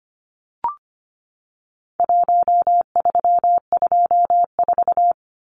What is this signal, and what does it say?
Morse code, heard as a keyed tone, about 25 wpm, 700 Hz. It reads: 1324
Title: Numbers - Contest Style - 4-Digit 25wpm